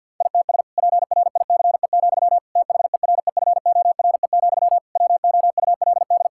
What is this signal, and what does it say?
Morse code, heard as a keyed tone, about 50 wpm, 700 Hz. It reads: ITS PRICE, THEREFORE, WOULD
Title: Sentences from Top 1000 Words - Review Part 5 - 50wpm